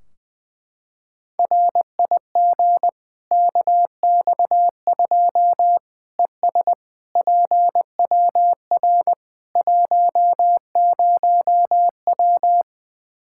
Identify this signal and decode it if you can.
Morse code, heard as a keyed tone, about 21 wpm, 700 Hz. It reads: RIG KX2 ES PWR 10W
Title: K1USN SST QSO Practice 20wpm